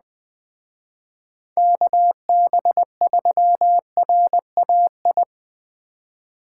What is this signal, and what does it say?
Morse code, heard as a keyed tone, about 20 wpm, 700 Hz. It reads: KB3RAI